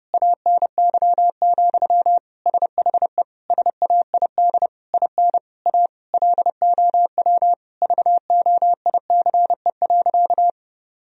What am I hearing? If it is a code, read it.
Morse code, heard as a keyed tone, about 30 wpm, 700 Hz. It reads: ANY, SHE SAID IN A LOW VOICE.
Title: Sentences from Top 1000 Words - Review Part 1 - 30wpm